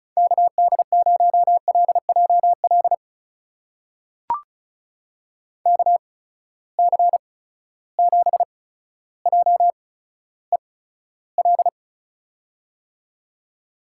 Morse code, heard as a keyed tone, about 36 wpm, spaced as for 15 wpm, 700 Hz. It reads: KD0LJL KC7JEL
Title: USA Call Signs - 6 Characters - 8x Character Spacing - 35wpm